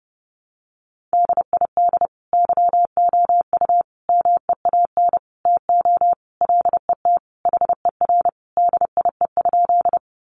Morse code, heard as a keyed tone, about 30 wpm, 700 Hz. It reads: DID YOU MEAN TO LET HER DIE?
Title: Sentences from Top 1300 Words 30wpm